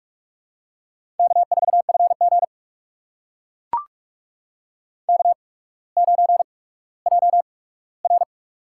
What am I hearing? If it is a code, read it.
Morse code, heard as a keyed tone, about 44 wpm, spaced as for 19 wpm, 690 Hz. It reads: K4FG K9JR
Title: USA Call Signs - 4 Characters - 8x Character Spacing - 45wpm